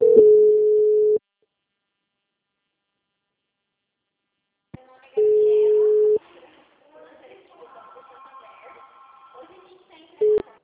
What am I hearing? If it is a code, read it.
Ringback tone (Europe, 425 Hz), with speech.